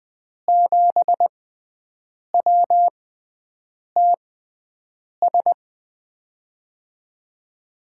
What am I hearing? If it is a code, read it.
Morse code, heard as a keyed tone, about 21 wpm, spaced as for 7 wpm, 700 Hz. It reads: 7WTS
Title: USA Call Signs - 5 Characters - 6x Character Spacing - 20wpm